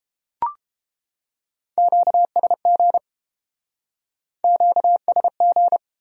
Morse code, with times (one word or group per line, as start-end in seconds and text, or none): 1.8-3.0 QSG
4.4-5.8 QSG